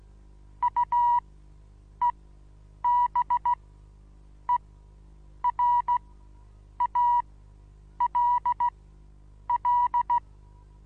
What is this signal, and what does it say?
Morse code, heard as a keyed tone, about 16 wpm, spaced as for 8 wpm, 970 Hz. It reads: UEBERALL
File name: Onlinelehrgang / morsen / DJ4UF Morsen25